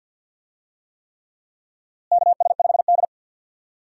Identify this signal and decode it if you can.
Morse code, heard as a keyed tone, about 52 wpm, 700 Hz. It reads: KS5D
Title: USA Call Signs - 4 Characters - Repeated 3x - 50wpm